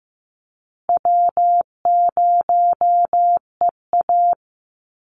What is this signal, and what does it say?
Morse code, heard as a keyed tone, about 15 wpm, 700 Hz. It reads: W0EA
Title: Easy Call Signs Repeated 3x - 15wpm